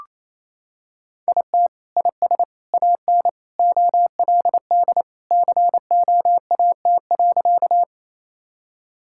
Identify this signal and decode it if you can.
Morse code, heard as a keyed tone, about 28 wpm, 700 Hz. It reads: IT IS AN OLD COAT.
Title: Sentences from Top 1600 Words 28wpm